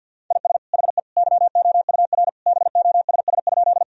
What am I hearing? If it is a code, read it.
Morse code, heard as a keyed tone, about 50 wpm, 700 Hz. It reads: IS HE YOUR BOSS?
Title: Sentences from Top 1000 Words - Review Part 4 - 50wpm